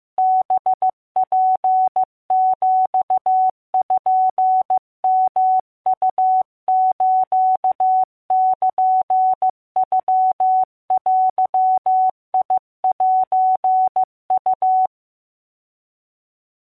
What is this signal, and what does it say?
Morse code (Japanese Wabun code), heard as a keyed tone, about 15 wpm, 750 Hz. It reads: ハツピヨウスルノデセウ